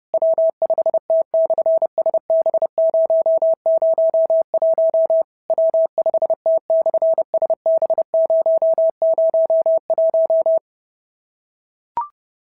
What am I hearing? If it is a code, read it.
Morse code, heard as a keyed tone, about 30 wpm, 690 Hz. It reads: W5T/SB001 W5T/SB001
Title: SOTA QSO Practice 30wpm